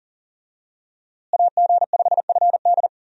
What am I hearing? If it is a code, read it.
Morse code, heard as a keyed tone, about 38 wpm, 700 Hz. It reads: AG5FD
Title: USA Call Signs - 5 Characters - Repeated 2x - 40wpm